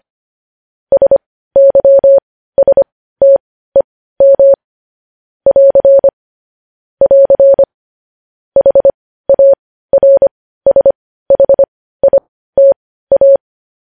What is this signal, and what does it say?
Morse code, heard as a keyed tone, about 25 wpm, spaced as for 15 wpm, 550 Hz. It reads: SYSTEM <AR> <AR> HARSHITA
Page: news-headlines-in-morse-code-at-15-wpm